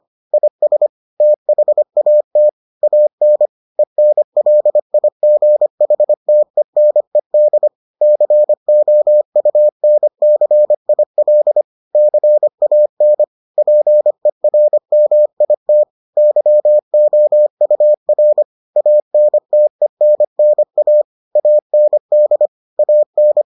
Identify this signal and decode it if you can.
Morse code, heard as a keyed tone, about 25 wpm, 600 Hz. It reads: IS THAT AN ENLIGHTENED COUNCIL CAN PERMIT YOUR ANTENNA AND AN